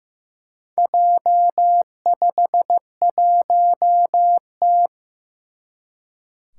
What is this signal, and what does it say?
Morse code, heard as a keyed tone, about 15 wpm, 700 Hz. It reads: J51T
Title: Four Challenging Letter-Number Combinations 15wpm